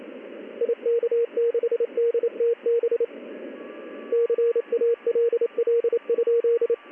Morse code, heard as a keyed tone, about 28 wpm, 470 Hz, with radio noise. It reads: IK6DTB CALL?
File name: contests / ss-ssb-k5tr- -1700